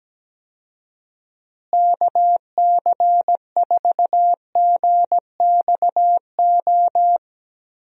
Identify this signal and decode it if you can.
Morse code, heard as a keyed tone, about 17 wpm, 700 Hz. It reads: KC4GXO
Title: USA Call Signs - 6 Characters - Repeated 3x - 17wpm